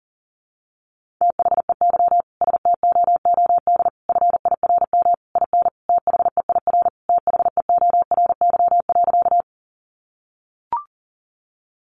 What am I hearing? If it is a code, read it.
Morse code, heard as a keyed tone, about 40 wpm, 700 Hz. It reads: THEY STOOD FIRM IN THEIR THEORY.
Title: Sentences from Top 1400 Words 40wpm